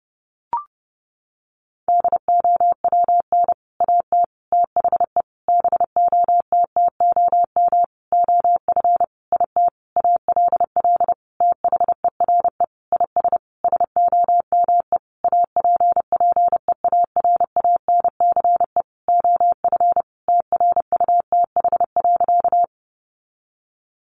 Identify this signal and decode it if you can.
Morse code, heard as a keyed tone, about 30 wpm, 700 Hz. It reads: DOWN AT THE BOTTOM OF IT ALL THERE IS SOME APPEARANCE OF TRUTH.
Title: Sentences from Top 1100 Words 30wpm